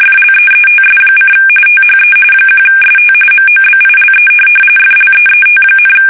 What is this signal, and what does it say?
RTTY, about 45.45 baud. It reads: FLEETS: THE 1ST MOBILE FLEET,
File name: fsk850 / USS-Massachusetts